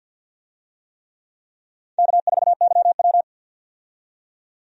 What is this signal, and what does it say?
Morse code, heard as a keyed tone, about 52 wpm, 700 Hz. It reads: K4YW
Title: USA Call Signs - 4 Characters - 50wpm